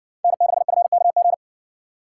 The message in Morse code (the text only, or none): N6FKG